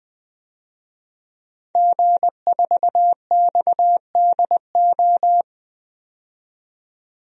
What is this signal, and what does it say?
Morse code, heard as a keyed tone, about 20 wpm, 700 Hz. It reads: G4XDO
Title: DXCC-100 Call Signs - 5 Characters - 20wpm